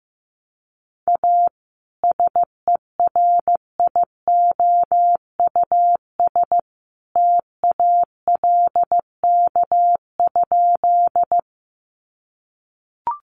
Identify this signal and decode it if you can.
Morse code, heard as a keyed tone, about 15 wpm, 700 Hz. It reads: A SERIOUS TALK?
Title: Sentences from Top 1100 Words 15wpm